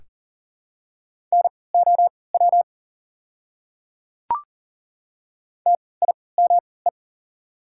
Morse code, heard as a keyed tone, about 40 wpm, spaced as for 23 wpm, 700 Hz. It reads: NOW TIME
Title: 3x Character Spacing - Top 100 Words - 40wpm